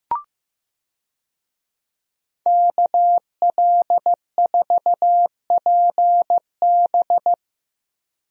Morse code, heard as a keyed tone, about 15 wpm, 700 Hz. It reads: KL4PB
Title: USA Call Signs - 5 Characters - Repeated 3x - 15wpm